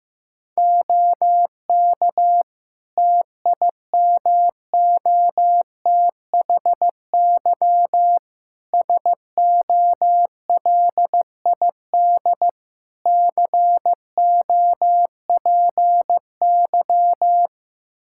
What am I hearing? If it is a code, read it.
Morse code, heard as a keyed tone, about 15 wpm, 700 Hz. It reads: OK TIMOTHY SOLID COPY
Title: K1USN SST QSO Practice 15wpm